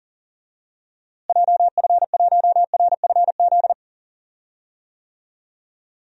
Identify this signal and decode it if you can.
Morse code, heard as a keyed tone, about 40 wpm, 700 Hz. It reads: JF1RFZ
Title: DXCC-100 Call Signs - 6 Characters - Repeated 2x - 40wpm